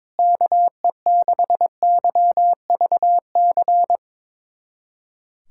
Morse code, heard as a keyed tone, about 21 wpm, 700 Hz. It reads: KE6YVC
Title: USA Callsign DE Callsign - 6 Characters - 22wpm